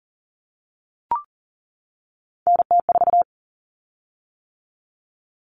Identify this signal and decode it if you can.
Morse code, heard as a keyed tone, about 40 wpm, 700 Hz. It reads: NT4
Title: Common Cut Numbers - 3-Digit 40wpm